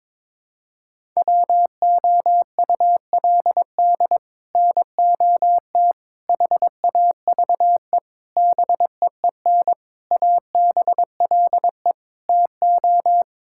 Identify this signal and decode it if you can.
Morse code, heard as a keyed tone, about 22 wpm, 700 Hz. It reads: WOULD NOT HAVE BEEN ABLE TO
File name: practice / 22wpm / Sets of 6 Words - 22wpm